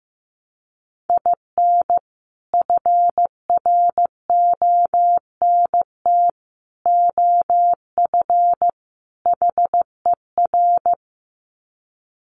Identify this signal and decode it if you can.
Morse code, heard as a keyed tone, about 15 wpm, 700 Hz. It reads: IN FRONT OF HER
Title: Sets of 4 Words 15wpm